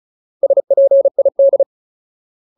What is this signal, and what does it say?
Morse code, heard as a keyed tone, about 34 wpm, 550 Hz. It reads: SPID
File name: Ultimate Word List / 35WPM Ultimate